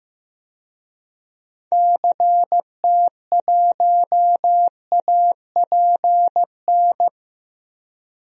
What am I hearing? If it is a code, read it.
Morse code, heard as a keyed tone, about 15 wpm, 700 Hz. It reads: CT1APN
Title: DXCC-100 Call Signs - 6 Characters - Repeated 2x - 15wpm